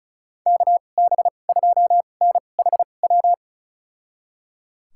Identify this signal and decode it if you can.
Morse code, heard as a keyed tone, about 34 wpm, spaced as for 25 wpm, 700 Hz. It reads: KB2NHW
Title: USA Call Signs - 6 Characters - 2x Character Spacing - 35wpm